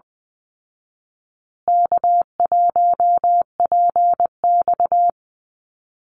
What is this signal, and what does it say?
Morse code, heard as a keyed tone, about 19 wpm, 700 Hz. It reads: K1PX
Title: Easy Call Signs Repeated 3x - 20wpm